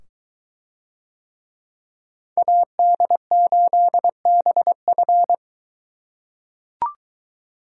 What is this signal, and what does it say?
Morse code, heard as a keyed tone, about 23 wpm, 700 Hz. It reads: AD8BF